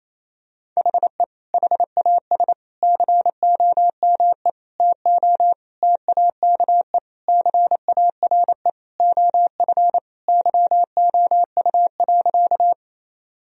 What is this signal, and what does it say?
Morse code, heard as a keyed tone, about 28 wpm, 700 Hz. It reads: HE HAS COME TO TAKE CARE OF YOU.
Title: Sentences from Top 1000 Words - Review Part 1 - 28wpm